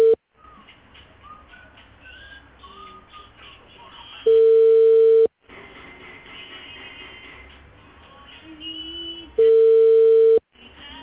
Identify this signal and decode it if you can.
Ringback tone (China, 450 Hz).